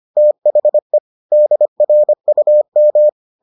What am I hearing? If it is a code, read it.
Morse code, heard as a keyed tone, about 25 wpm, 600 Hz. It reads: THE DRUM